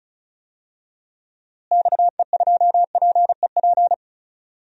Morse code, heard as a keyed tone, about 36 wpm, 700 Hz. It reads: XE2PEP
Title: DXCC-100 Call Signs - 6 Characters - Repeated 2x - 35wpm